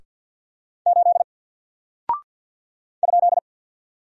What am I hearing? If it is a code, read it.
Morse code, heard as a keyed tone, about 49 wpm, 700 Hz. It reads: 8 ?
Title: Single Character (Letter:Number:Punctuation:Prosign) Alt 50wpm